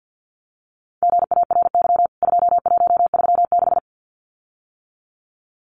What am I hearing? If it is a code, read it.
Morse code, heard as a keyed tone, about 50 wpm, 700 Hz. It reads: GARY 2136
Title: CWT Exchange 50wpm